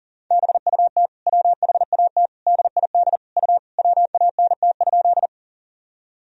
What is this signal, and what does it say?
Morse code, heard as a keyed tone, about 40 wpm, 700 Hz. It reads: BUT WHAT DID U WANT?
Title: Sentences from Top 200 Words and Abbreviations 40wpm